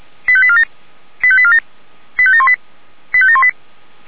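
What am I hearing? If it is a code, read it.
Five-tone selective call (ZVEI), heard as quick runs of five tones, tones of about 70 ms. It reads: 85847 85847 85718 85718
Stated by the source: ZVEI five-tone selective call received on SDR.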